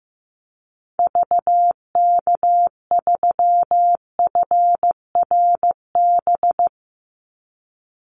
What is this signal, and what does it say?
Morse code, heard as a keyed tone, about 15 wpm, 700 Hz. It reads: VK3FRB
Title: USA Call Signs 15wpm